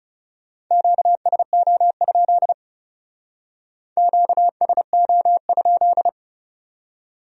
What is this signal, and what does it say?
Morse code, heard as a keyed tone, about 39 wpm, 700 Hz. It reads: QSO? QSO?